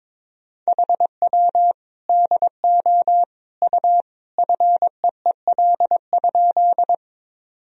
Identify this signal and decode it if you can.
Morse code, heard as a keyed tone, about 22 wpm, 700 Hz. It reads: HW DO U FEEL?
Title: Sentences from Top 400 Words and Abbreviations 22wpm